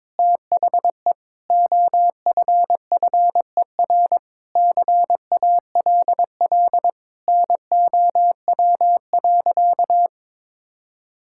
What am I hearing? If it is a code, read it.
Morse code, heard as a keyed tone, about 22 wpm, 700 Hz. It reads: THE OFFER CALL NOW.
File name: practice / 22wpm / Sentences with 4-Word Phrases - 22wpm